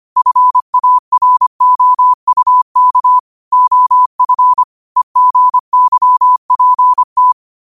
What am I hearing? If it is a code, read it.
Morse code, heard as a keyed tone, about 25 wpm, 1000 Hz. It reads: FAROUK OF EGYPT
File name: qotdfiles / 25wpm